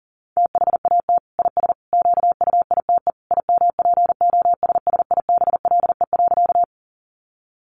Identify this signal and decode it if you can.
Morse code, heard as a keyed tone, about 40 wpm, 700 Hz. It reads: THAT IS QUITE IMPOSSIBLE.
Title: Sentences from Top 1100 Words 40wpm